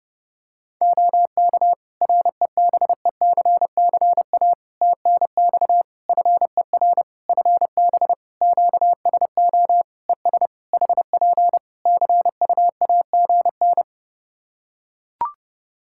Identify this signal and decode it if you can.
Morse code, heard as a keyed tone, about 30 wpm, 700 Hz. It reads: OK REBECCA TNX FER FB QSO ES HP CUAGN
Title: Practice QSO - Exchange 4 - Ending - 30wpm